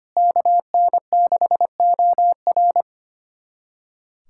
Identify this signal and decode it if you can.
Morse code, heard as a keyed tone, about 25 wpm, 700 Hz. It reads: KN6OR
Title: USA Callsign DE Callsign 25wpm